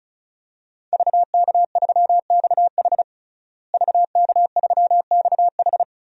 Morse code, heard as a keyed tone, about 36 wpm, 700 Hz. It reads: VK3XH VK3XH
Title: DXCC-100 Call Signs - 5 Characters - Repeated 3x - 35wpm